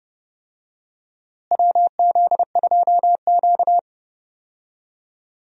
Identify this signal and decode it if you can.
Morse code, heard as a keyed tone, about 32 wpm, 700 Hz. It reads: WZ2Q
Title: USA Call Signs - 4 Characters - Repeated 2x - 30wpm